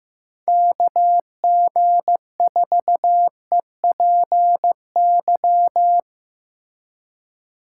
Morse code, heard as a keyed tone, about 17 wpm, 700 Hz. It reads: KG4EPY
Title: Speed Racing - USA 2x3 Call Signs 15wpm